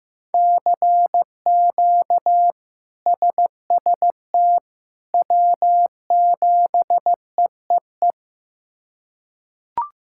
Morse code, heard as a keyed tone, about 16 wpm, 700 Hz. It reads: CQ SST W7EEE
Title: K1USN SST QSO Practice - 15wpm